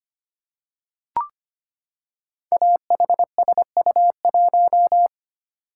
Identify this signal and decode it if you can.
Morse code, heard as a keyed tone, about 25 wpm, 700 Hz. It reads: AHSU1
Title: Code Groups - 5 Letter:Numbers - 25wpm